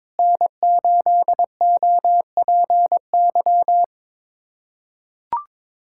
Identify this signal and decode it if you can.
Morse code, heard as a keyed tone, about 22 wpm, 700 Hz. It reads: N8OPY
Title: USA Callsign DE Callsign - 5 Characters - 22wpm